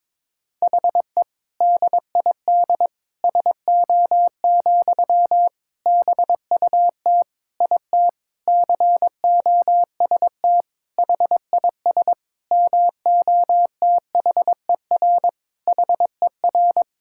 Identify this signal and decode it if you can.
Morse code, heard as a keyed tone, about 22 wpm, 700 Hz. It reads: HE DID SO, BUT IT COST HIS MOTHER HER
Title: Sentences from Top 1000 Words - Review Part 5 - 22wpm